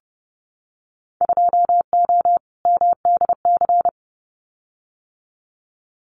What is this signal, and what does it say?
Morse code, heard as a keyed tone, about 30 wpm, 700 Hz. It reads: 2O MDC
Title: Winter Field Day Exchanges 30wpm